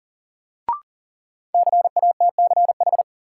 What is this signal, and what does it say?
Morse code, heard as a keyed tone, about 40 wpm, 700 Hz. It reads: CATCH
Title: Sentences from Top 1500 Words - New Word Drill Alt - 40wpm